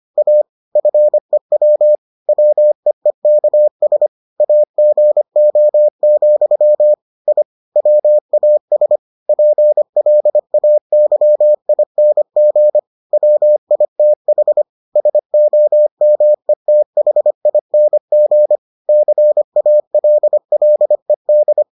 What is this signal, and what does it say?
Morse code, heard as a keyed tone, about 25 wpm, 600 Hz. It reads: A FEW WEEKS AGO, I WAS PLAYING WITH SOMETHING CALLED